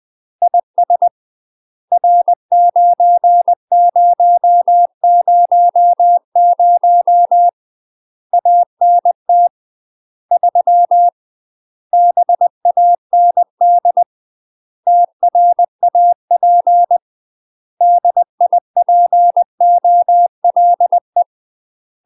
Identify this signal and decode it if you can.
Morse code, heard as a keyed tone, about 20 wpm, 700 Hz. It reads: IS R9000 ANT 3 BAND TRAP DIPOLE
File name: UK MASTER QSOs TRAINING / 20WPM / UK MASTER QSOs 0065